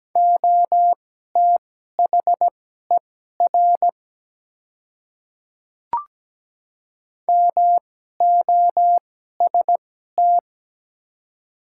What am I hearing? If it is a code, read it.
Morse code, heard as a keyed tone, about 17 wpm, spaced as for 12 wpm, 700 Hz. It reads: OTHER MOST
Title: 2x Character Spacing - Top 200 Words - 17wpm